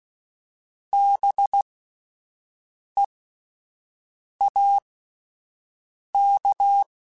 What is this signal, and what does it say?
Morse code, heard as a keyed tone, about 16 wpm, spaced as for 5 wpm, 800 Hz. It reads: BEAK